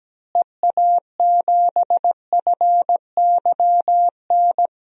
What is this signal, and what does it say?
Morse code, heard as a keyed tone, about 17 wpm, 700 Hz. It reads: EA7FYN
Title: DXCC-100 Call Signs - 6 Characters - Repeated 2x - 17wpm